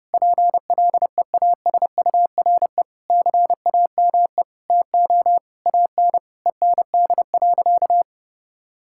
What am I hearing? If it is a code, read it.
Morse code, heard as a keyed tone, about 32 wpm, 700 Hz. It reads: PLEASURE CAME TO AN END.
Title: Sentences from Top 1600 Words 30wpm